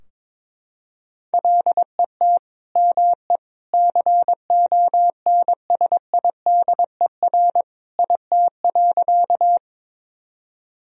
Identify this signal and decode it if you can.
Morse code, heard as a keyed tone, about 21 wpm, 700 Hz. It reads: LET ME CONSIDER IT.
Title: Sentences from Top 1000 Words - Review Part 4 - 22wpm